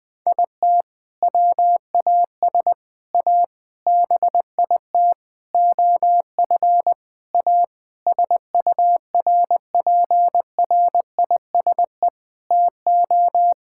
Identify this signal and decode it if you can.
Morse code, heard as a keyed tone, about 21 wpm, 700 Hz. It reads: IT WAS A BIT OF A SURPRISE TO
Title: Sentences with 6-Word Phrases - Spoken Beginning - 20wpm